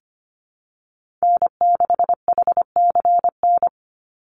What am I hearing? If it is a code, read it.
Morse code, heard as a keyed tone, about 25 wpm, 700 Hz. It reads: N6HCN
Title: Call Signs Repeated 2x - 25wpm